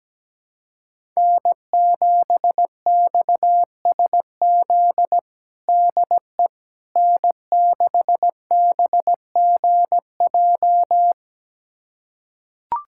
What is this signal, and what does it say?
Morse code, heard as a keyed tone, about 17 wpm, 700 Hz. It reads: N7XSZ DE N6BGJ
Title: USA Callsign DE Callsign - 5 Characters - 17wpm